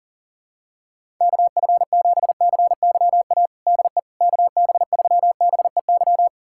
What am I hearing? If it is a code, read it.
Morse code, heard as a keyed tone, about 40 wpm, 700 Hz. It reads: KF7CYA DE KB3BEY
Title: USA Callsign DE Callsign - 6 Characters - 40wpm